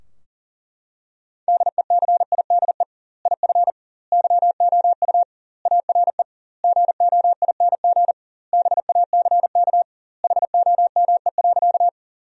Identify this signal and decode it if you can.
Morse code, heard as a keyed tone, about 37 wpm, 700 Hz. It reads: DECIDE IF YOU ARE GOING BACK HOME.
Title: 2x Word-Spacing - Sentences from Top 500 Words - 40wpm